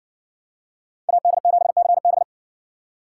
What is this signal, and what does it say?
Morse code, heard as a keyed tone, about 57 wpm, 700 Hz. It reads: AD7CB